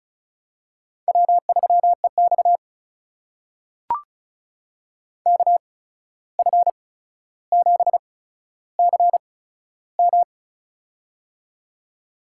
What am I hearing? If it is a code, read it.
Morse code, heard as a keyed tone, about 36 wpm, spaced as for 14 wpm, 700 Hz. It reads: W3EX KF7CM
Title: USA Call Signs - 8x Character Spacing - 35wpm